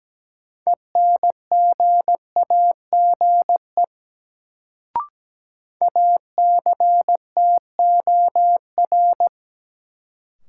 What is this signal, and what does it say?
Morse code, heard as a keyed tone, about 17 wpm, 700 Hz. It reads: ENGAGE ACTOR
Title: Sentences from Top 1500 Words - New Word Drill Alt - 17wpm